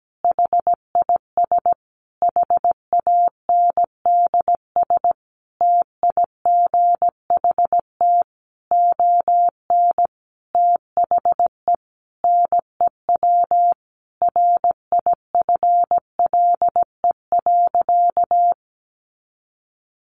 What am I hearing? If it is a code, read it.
Morse code, heard as a keyed tone, about 17 wpm, 700 Hz. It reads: HIS HANDS TIGHT ON THE NEW RIFLE.
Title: A Sound Of Thunder 17wpm